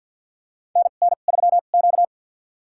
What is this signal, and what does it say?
Morse code, heard as a keyed tone, about 48 wpm, spaced as for 35 wpm, 700 Hz. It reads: NN3Q
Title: USA Call Signs - 4 Characters - 2x Character Spacing - 50wpm